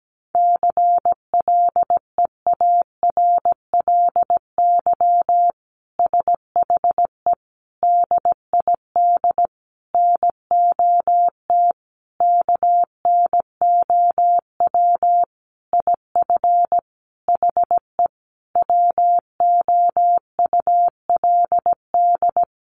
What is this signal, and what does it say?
Morse code, heard as a keyed tone, about 17 wpm, 700 Hz. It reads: CLEARLY SHE DID NOT KNOW IF HE WOULD
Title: Sentences from Top 1300 Words 17wpm